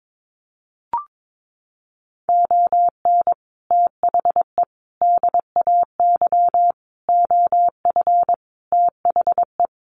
Morse code, heard as a keyed tone, about 22 wpm, 700 Hz. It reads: ON THE DAY OF THE
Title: Sets of 5 Words Encore 22wpm